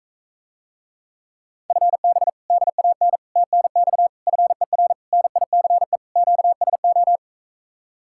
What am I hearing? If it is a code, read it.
Morse code, heard as a keyed tone, about 45 wpm, 700 Hz. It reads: FB DAN TNX FER NICE QSO